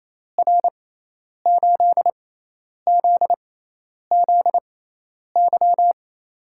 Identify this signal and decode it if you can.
Morse code, heard as a keyed tone, about 28 wpm, spaced as for 10 wpm, 700 Hz. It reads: R8ZZY
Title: USA Call Signs - 6 Characters - 6x Character Spacing - 28wpm